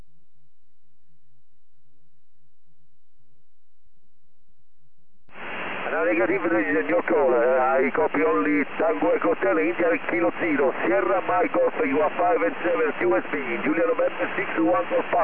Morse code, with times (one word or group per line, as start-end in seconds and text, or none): none